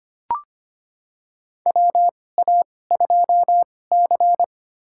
Morse code, heard as a keyed tone, about 25 wpm, spaced as for 18 wpm, 700 Hz. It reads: WA2C
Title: USA Call Signs - 4 Characters - 2x Character Spacing - 25wpm